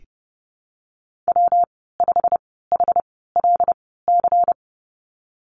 Morse code, heard as a keyed tone, about 30 wpm, spaced as for 17 wpm, 700 Hz. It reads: W5HLC
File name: Lesson 19g - Letter w - Call Signs 30wpm